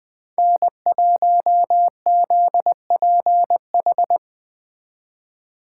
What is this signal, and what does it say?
Morse code, heard as a keyed tone, about 20 wpm, 700 Hz. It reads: N1ZPH
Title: Speed Racing - USA 1x3 Call Signs 17wpm